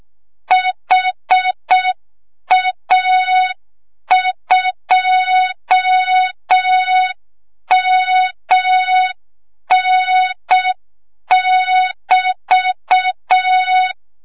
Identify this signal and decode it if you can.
Morse code, heard as a keyed tone, about 6 wpm, 3700 Hz. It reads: HA2MN<BT>
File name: Morze-HA2MN / hang / 25 GYAKORLAT